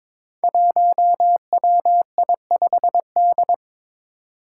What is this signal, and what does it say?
Morse code, heard as a keyed tone, about 22 wpm, 700 Hz. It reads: 1WI5D
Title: Five Challenging Letter-Number Combinations 22wpm